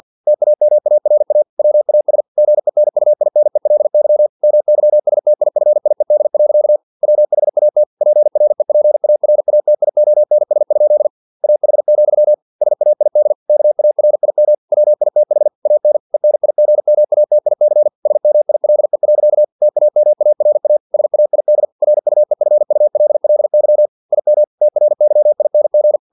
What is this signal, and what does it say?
Morse code, heard as a keyed tone, about 50 wpm, 600 Hz. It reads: TAMARA WAS GENUINELY MYSTIFIED. WHAT PREPARATIONS? AH, SAID KARIM WITH AN ENIGMATIC SMILE. TAMARA SAID RUEFULLY IM TRYING
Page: Never 49wpm